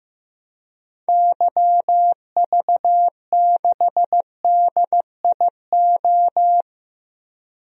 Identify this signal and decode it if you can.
Morse code, heard as a keyed tone, about 15 wpm, 700 Hz. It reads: YV6DIO
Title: DXCC-100 Call Signs - 6 Characters - 15wpm